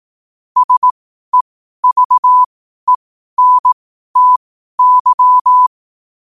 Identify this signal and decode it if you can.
Morse code, heard as a keyed tone, about 18 wpm, spaced as for 12 wpm, 1000 Hz. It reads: SEVENTY
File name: qotdfiles / 12wpm